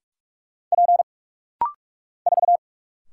Morse code, heard as a keyed tone, about 44 wpm, 700 Hz. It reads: P 4
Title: Single Letter-Number Alt - Rapid-Fire 45wpm